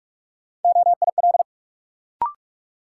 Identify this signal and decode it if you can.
Morse code, heard as a keyed tone, about 46 wpm, 700 Hz. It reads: OIL
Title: Sentences from Top 1000 Words - Review Part 4 - New Word Drill - 45wpm